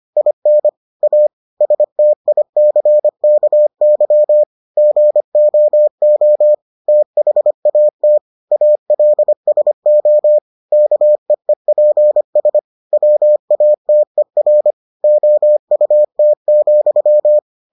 Morse code, heard as a keyed tone, about 25 wpm, 600 Hz. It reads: IN A STICKY GOO THAT ALSO KEEPS WATER OUT,